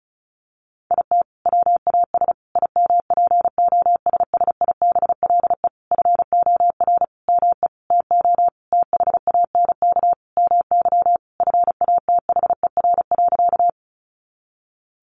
Morse code, heard as a keyed tone, about 34 wpm, 700 Hz. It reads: IT WAS IMPOSSIBLE FOR ME TO THANK MY FATHER.
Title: Sentences from Top 1100 Words 35wpm